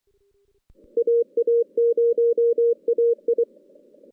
Morse code, heard as a keyed tone, about 27 wpm, 440 Hz. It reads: AA0AI
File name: 2016 JAN CW NAQP / 2016 Jan naqp k5ot - 0-5